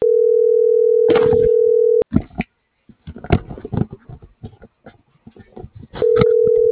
Ringback tone (North America), 440 + 480 Hz.